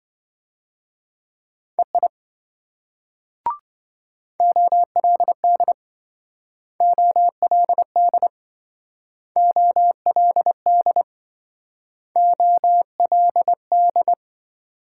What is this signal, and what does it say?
Morse code, heard as a keyed tone, about 26 wpm, 700 Hz. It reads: EI OLD OLD OLD OLD